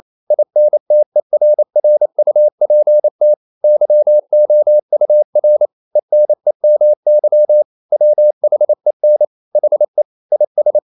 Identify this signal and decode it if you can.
Morse code, heard as a keyed tone, about 28 wpm, 600 Hz. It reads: INTERRUPT YOUR ENEMY WHEN HE IS
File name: CW / Practice / EnglishQuotations0128